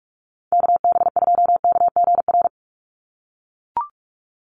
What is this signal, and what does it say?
Morse code, heard as a keyed tone, about 44 wpm, 700 Hz. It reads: KB2KGR
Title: USA Call Signs 45wpm